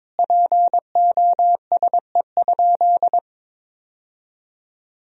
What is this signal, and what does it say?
Morse code, heard as a keyed tone, about 22 wpm, 700 Hz. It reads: POSE?
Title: Sentences from Top 1000 Words - Review Part 2 - 22wpm